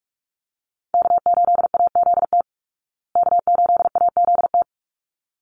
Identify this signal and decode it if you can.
Morse code, heard as a keyed tone, about 44 wpm, 700 Hz. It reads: K8AZT K8AZT
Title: Call Signs Repeated 3x - 45wpm